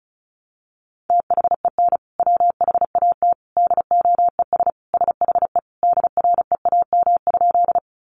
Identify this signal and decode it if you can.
Morse code, heard as a keyed tone, about 34 wpm, 700 Hz. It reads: THEN WHAT DOES SHE DREAM?
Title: Sentences from Top 1500 Words 35wpm